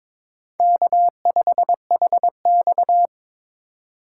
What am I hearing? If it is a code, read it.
Morse code, heard as a keyed tone, about 26 wpm, 700 Hz. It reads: K5HX
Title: Speed Racing - USA 1x2 Call Signs 22wpm